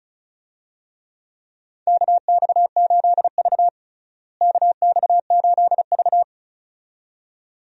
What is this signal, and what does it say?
Morse code, heard as a keyed tone, about 36 wpm, 700 Hz. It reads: KX8V KX8V
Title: USA Call Signs - 4 Characters - Repeated 2x - 35wpm